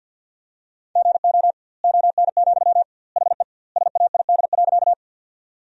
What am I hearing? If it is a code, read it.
Morse code, heard as a keyed tone, about 52 wpm, 700 Hz. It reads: GO ON, HE SAID.
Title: 2x Word-Spacing - Sentences from Top 100 Words - 50wpm